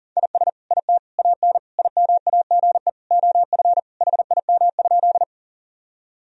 Morse code, heard as a keyed tone, about 37 wpm, 700 Hz. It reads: IS IT AN IMAGE OF HIM?
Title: Sentences from Top 1000 Words - Review Part 4 - 40wpm